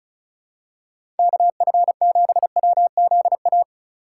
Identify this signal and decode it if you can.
Morse code, heard as a keyed tone, about 36 wpm, 700 Hz. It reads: KF7WZA
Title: USA Call Signs - 6 Characters - 4x Character Spacing - 35wpm